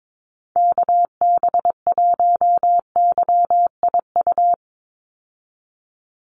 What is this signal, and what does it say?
Morse code, heard as a keyed tone, about 22 wpm, 700 Hz. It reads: KB1YIU
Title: USA Call Signs 22wpm